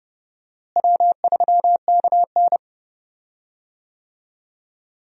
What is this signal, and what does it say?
Morse code, heard as a keyed tone, about 30 wpm, 700 Hz. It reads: W3KN
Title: USA Call Signs - 4 Characters - Repeated 2x - 30wpm